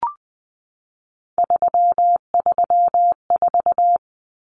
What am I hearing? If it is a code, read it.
Morse code, heard as a keyed tone, about 20 wpm, 700 Hz. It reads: 334
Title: Numbers - 3-Digit 20wpm